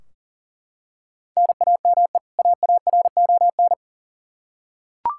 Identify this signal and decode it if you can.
Morse code, heard as a keyed tone, about 40 wpm, 700 Hz. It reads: NAME AARON
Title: Practice QSO - Exchange 1 - Beginning - 40wpm